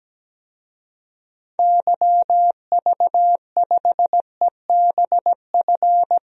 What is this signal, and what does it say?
Morse code, heard as a keyed tone, about 17 wpm, 700 Hz. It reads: YV5EBF
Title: DXCC-100 Call Signs - 6 Characters - Repeated 2x - 17wpm